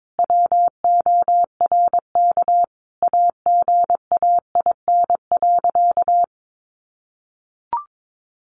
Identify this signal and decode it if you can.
Morse code, heard as a keyed tone, about 22 wpm, 700 Hz. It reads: WORK AGAIN.
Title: Sentences from Top 1500 Words 22wpm